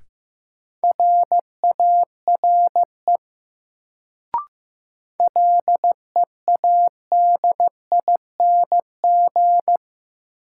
Morse code, heard as a keyed tone, about 15 wpm, 700 Hz. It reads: RARE LEADING
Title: Sentences from Top 1100 Words - New Word Drill - 15wpm